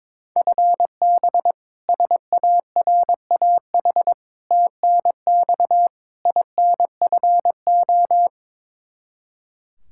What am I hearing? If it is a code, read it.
Morse code, heard as a keyed tone, about 22 wpm, 700 Hz. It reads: FB SARAH TNX INFO
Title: K1USN SST QSO Practice 22wpm